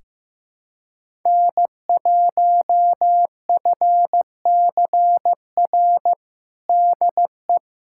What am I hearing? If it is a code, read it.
Morse code, heard as a keyed tone, about 15 wpm, 700 Hz. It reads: N1FCR DE
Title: USA Callsign DE Callsign - 5 Characters - 15wpm